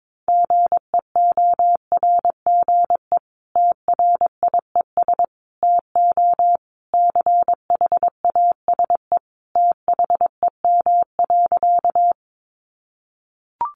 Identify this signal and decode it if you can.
Morse code, heard as a keyed tone, about 22 wpm, 700 Hz. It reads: GEORGE TRIES TO CHASE THEM.
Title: Additional Easy Phrases 22wpm